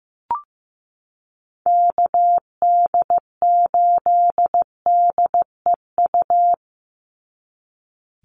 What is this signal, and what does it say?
Morse code, heard as a keyed tone, about 15 wpm, 700 Hz. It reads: KD8DEU
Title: NA SOTA Chaser Callsigns 15wpm